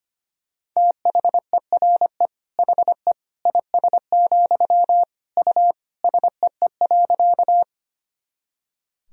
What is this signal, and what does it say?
Morse code, heard as a keyed tone, about 25 wpm, 700 Hz. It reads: THERE HE IS, U SEE.
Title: Sentences from Top 100 Words and Abbreviations 25wpm